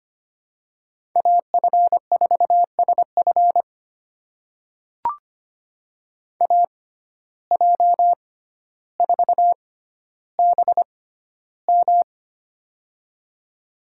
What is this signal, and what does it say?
Morse code, heard as a keyed tone, about 25 wpm, spaced as for 13 wpm, 700 Hz. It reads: AF4SF AJ4BM
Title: USA Call Signs - 5 Characters - 6x Character Spacing - 25wpm